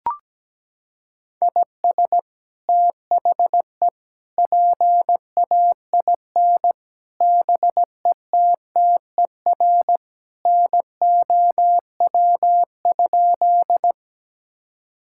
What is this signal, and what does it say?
Morse code, heard as a keyed tone, about 17 wpm, 700 Hz. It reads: IS THE PAIN BETTER NOW?
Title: Sentences from Top 1600 Words 17wpm